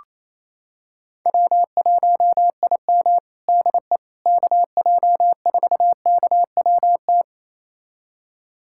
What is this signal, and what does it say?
Morse code, heard as a keyed tone, about 28 wpm, 700 Hz. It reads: W1IM DE KJ4KWT
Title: USA Callsign DE Callsign 28wpm